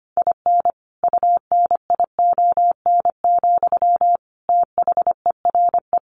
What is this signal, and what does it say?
Morse code, heard as a keyed tone, about 25 wpm, 700 Hz. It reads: IN UNION, THERE
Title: Sentences from Top 1400 Words 25wpm